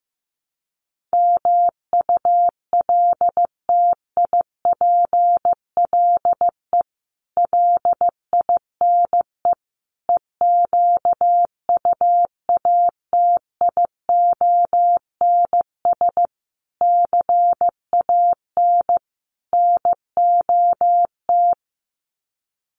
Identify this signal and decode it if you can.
Morse code, heard as a keyed tone, about 15 wpm, 700 Hz. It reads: MULTIPLE LINE EQUATIONS CAN NOT
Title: Sets of 5 Words 15wpm